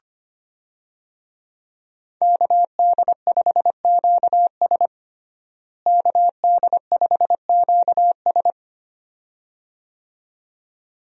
Morse code, heard as a keyed tone, about 25 wpm, 700 Hz. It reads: KD5QS KD5QS
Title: USA Call Signs - 5 Characters - Repeated 2x - 25wpm